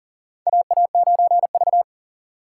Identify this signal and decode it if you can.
Morse code, heard as a keyed tone, about 37 wpm, 700 Hz. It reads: AA9V
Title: USA Call Signs - 4 Characters - 6x Character Spacing - 40wpm